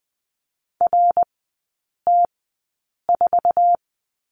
Morse code, heard as a keyed tone, about 20 wpm, spaced as for 8 wpm, 700 Hz. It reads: RT4
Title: 074 Lesson 13e - Letter L - Three Characters 20wpm